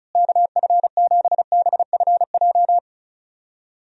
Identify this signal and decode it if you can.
Morse code, heard as a keyed tone, about 36 wpm, 700 Hz. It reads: KF7BFJ
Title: USA Call Signs - 6 Characters - Repeated 2x - 35wpm